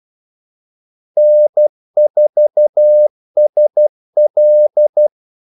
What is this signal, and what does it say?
Morse code, heard as a keyed tone, about 12 wpm, 600 Hz. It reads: N4SL
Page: 2 ekstra ordmellemrum mellem kaldesignalerne